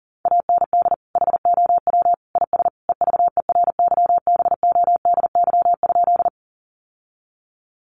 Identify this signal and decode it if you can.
Morse code, heard as a keyed tone, about 40 wpm, 700 Hz. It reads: AND HOW IS EVERYBODY?
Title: Sentences from Top 1400 Words 40wpm